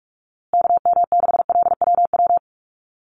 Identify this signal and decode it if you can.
Morse code, heard as a keyed tone, about 44 wpm, 700 Hz. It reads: KM6LWW
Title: USA Call Signs Repeated 2x 45wpm